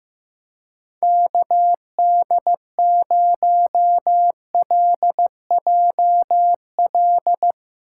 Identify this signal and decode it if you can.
Morse code, heard as a keyed tone, about 15 wpm, 700 Hz. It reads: KD0LJL
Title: USA Call Signs - 6 Characters - 6x Character Spacing - 15wpm